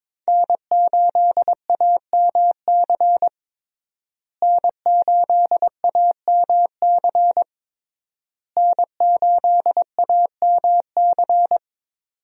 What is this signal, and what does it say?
Morse code, heard as a keyed tone, about 22 wpm, 700 Hz. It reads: N8AMC N8AMC N8AMC
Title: USA Call Signs - 5 Characters - Repeated 3x - 22wpm